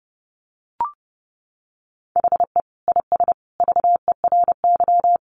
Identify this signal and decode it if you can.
Morse code, heard as a keyed tone, about 30 wpm, 700 Hz. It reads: HE IS VERY
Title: Easy Phrases 30wpm